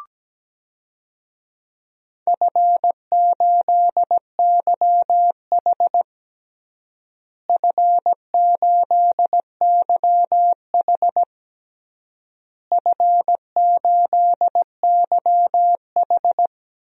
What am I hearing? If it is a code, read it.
Morse code, heard as a keyed tone, about 17 wpm, 700 Hz. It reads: F8YH F8YH F8YH
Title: DXCC-100 Call Signs - 4 Characters - Repeated 3x - 17wpm